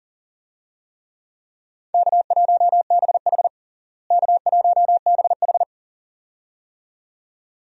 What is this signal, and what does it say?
Morse code, heard as a keyed tone, about 40 wpm, 700 Hz. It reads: K1BH K1BH
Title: USA Call Signs - 4 Characters - Repeated 2x - 40wpm